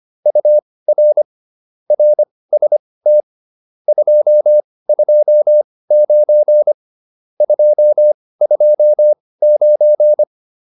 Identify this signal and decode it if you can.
Morse code, heard as a keyed tone, about 25 wpm, spaced as for 18 wpm, 600 Hz. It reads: UR RST 229 229